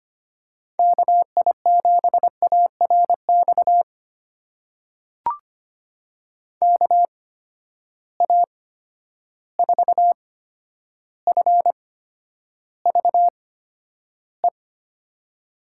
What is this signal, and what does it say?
Morse code, heard as a keyed tone, about 25 wpm, spaced as for 11 wpm, 700 Hz. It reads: KI7ARX KA4FVE
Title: USA Call Signs - 6 Characters - 8x Character Spacing - 25wpm